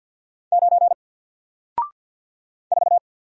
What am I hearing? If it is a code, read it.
Morse code, heard as a keyed tone, about 48 wpm, 700 Hz. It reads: 9 4
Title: Single Letter-Number Alt 50wpm